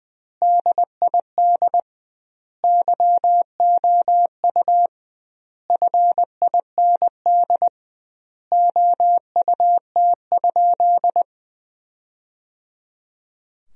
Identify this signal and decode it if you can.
Morse code, heard as a keyed tone, about 20 wpm, 700 Hz. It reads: DID YOU FIND OUT?
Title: 2x Word-Spacing - Sentences from Top 200 Words - 20wpm